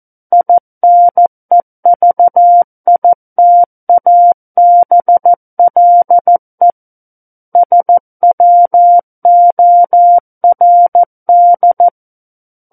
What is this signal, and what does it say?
Morse code, heard as a keyed tone, about 14 wpm, 700 Hz. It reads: INEVITABLE SWORD
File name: Book One / 05-Chapter 5 - 14 WPM